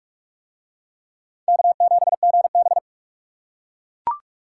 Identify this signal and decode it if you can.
Morse code, heard as a keyed tone, about 37 wpm, 700 Hz. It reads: K7GB
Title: Speed Racing - USA 1x2 Call Signs 30wpm